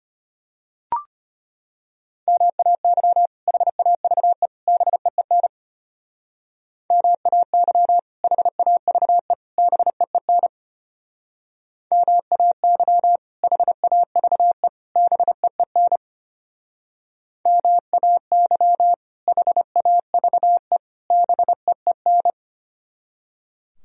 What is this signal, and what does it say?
Morse code, heard as a keyed tone, about 32 wpm, 700 Hz. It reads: MAY HAVE BEEN MAY HAVE BEEN MAY HAVE BEEN MAY HAVE BEEN